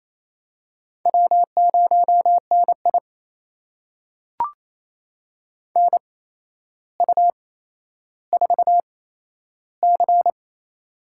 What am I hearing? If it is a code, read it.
Morse code, heard as a keyed tone, about 28 wpm, spaced as for 12 wpm, 700 Hz. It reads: W0NI NU4C
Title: USA Call Signs - 4 Characters - 8x Character Spacing - 28wpm